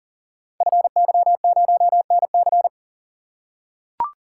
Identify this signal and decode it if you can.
Morse code, heard as a keyed tone, about 38 wpm, 690 Hz. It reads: FY0NC
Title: Code Groups - 5 Letter:Numbers - 40wpm